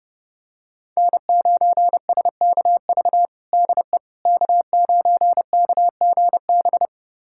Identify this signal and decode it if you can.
Morse code, heard as a keyed tone, about 32 wpm, 700 Hz. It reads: N9SKV DE K9KGB
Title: USA Callsign DE Callsign - 5 Characters - 30wpm